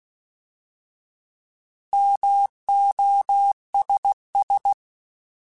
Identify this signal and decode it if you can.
Morse code, heard as a keyed tone, about 16 wpm, 800 Hz. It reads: MOSS
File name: Alphabetical / Alphabetical mp3s / Ms @15